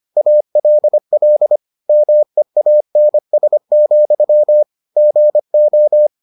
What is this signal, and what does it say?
Morse code, heard as a keyed tone, about 25 wpm, 600 Hz. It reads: ALL MEANS, GO